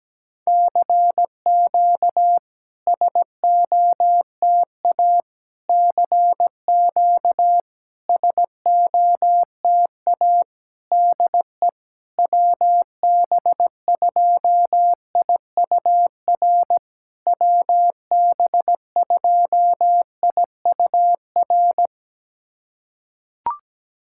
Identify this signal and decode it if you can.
Morse code, heard as a keyed tone, about 17 wpm, 700 Hz. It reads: CQ SOTA CQ SOTA DE WB2IUR WB2IUR
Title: SOTA QSO Practice 17wpm